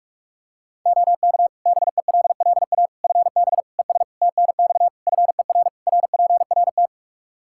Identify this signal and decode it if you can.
Morse code, heard as a keyed tone, about 46 wpm, 700 Hz. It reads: OK BELLA FB ES TNX FER RPRT
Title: K1USN SST QSO Practice 45wpm